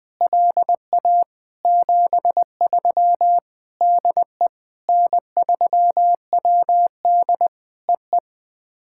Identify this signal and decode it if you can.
Morse code, heard as a keyed tone, about 20 wpm, 700 Hz. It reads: LA 73 DE N3WD EE
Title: POTA QSO Practice - 20wpm